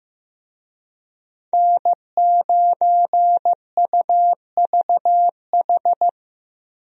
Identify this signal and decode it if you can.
Morse code, heard as a keyed tone, about 15 wpm, 700 Hz. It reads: N9UVH
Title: USA Call Signs - 5 Characters - Repeated 3x - 15wpm